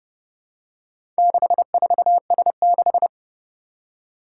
Morse code, heard as a keyed tone, about 29 wpm, 700 Hz. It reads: 64S6
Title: Four Challenging Letter-Number Combinations 30wpm